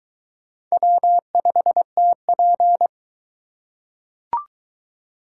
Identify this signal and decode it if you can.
Morse code, heard as a keyed tone, about 20 wpm, 700 Hz. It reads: W5TP